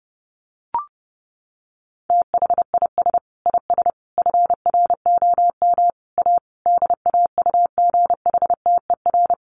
Morse code, heard as a keyed tone, about 30 wpm, 700 Hz. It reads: THIS IS FROM A DAUGHTER
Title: Sentences from Top 1500 Words 30wpm